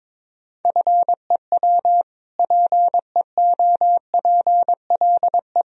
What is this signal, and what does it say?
Morse code, heard as a keyed tone, about 22 wpm, 700 Hz. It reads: FEW PEOPLE
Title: Sentences from Top 1000 Words - Review Part 1 - 22wpm